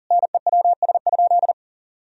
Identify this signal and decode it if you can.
Morse code, heard as a keyed tone, about 40 wpm, 700 Hz. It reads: NEWS?
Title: Sentences from Top 1000 Words - Review Part 4 - 40wpm